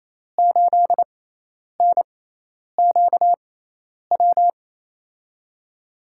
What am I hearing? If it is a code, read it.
Morse code, heard as a keyed tone, about 28 wpm, spaced as for 10 wpm, 700 Hz. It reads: 8NQW
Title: USA Call Signs - 6 Characters - 6x Character Spacing - 28wpm